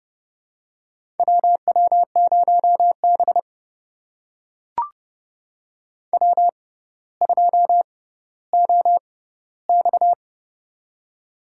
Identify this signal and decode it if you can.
Morse code, heard as a keyed tone, about 30 wpm, spaced as for 15 wpm, 690 Hz. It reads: WW0B W2OX
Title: USA Call Signs - 4 Characters - 6x Character Spacing - 30wpm